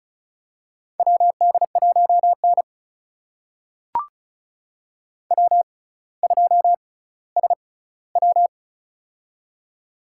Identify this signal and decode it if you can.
Morse code, heard as a keyed tone, about 36 wpm, spaced as for 18 wpm, 700 Hz. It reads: WD1N W2SW
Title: USA Call Signs - 4 Characters - 6x Character Spacing - 35wpm